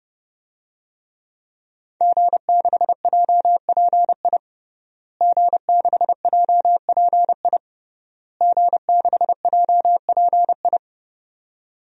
Morse code, heard as a keyed tone, about 32 wpm, 700 Hz. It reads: G6JPI G6JPI G6JPI
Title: DXCC-100 Call Signs - 5 Characters - Repeated 3x - 30wpm